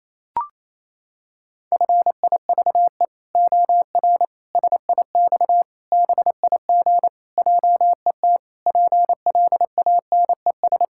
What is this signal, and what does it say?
Morse code, heard as a keyed tone, about 27 wpm, 700 Hz. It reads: FIVE OR SIX BIG JET PLANES
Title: Pangram Challenge v2 - 28wpm